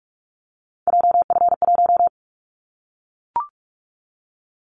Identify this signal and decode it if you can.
Morse code, heard as a keyed tone, about 44 wpm, 700 Hz. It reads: JF1
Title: Callsign Prefixes for Top 25 DXCC - Repeated 3x - 45wpm